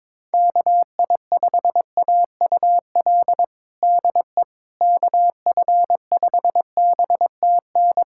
Morse code, heard as a keyed tone, about 21 wpm, 700 Hz. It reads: KI5AUL DE KF5BTN
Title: USA Callsign DE Callsign - 6 Characters - 22wpm